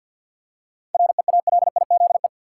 Morse code, heard as a keyed tone, about 53 wpm, 700 Hz. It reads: REALIZE
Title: Seven Letter Words - 50wpm